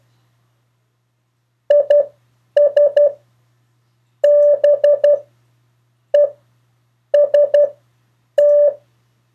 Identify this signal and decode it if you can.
Morse code, heard as a keyed tone, about 12 wpm, spaced as for 7 wpm, 580 Hz. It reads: IS BEST
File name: 4of4-sentences-mistakes-test